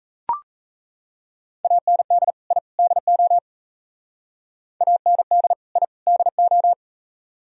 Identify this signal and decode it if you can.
Morse code, heard as a keyed tone, about 35 wpm, 700 Hz. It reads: AND I DO AND I DO